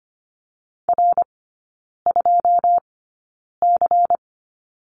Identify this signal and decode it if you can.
Morse code, heard as a keyed tone, about 25 wpm, spaced as for 9 wpm, 700 Hz. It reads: R2C
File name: course / 25:10wpm / 093 Lesson 16e - Letter c - Three Characters 25wpm